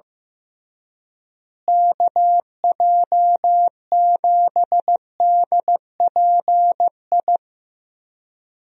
Morse code, heard as a keyed tone, about 15 wpm, 700 Hz. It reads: KJ7DPI
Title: USA Call Signs - 6 Characters - Repeated 3x - 15wpm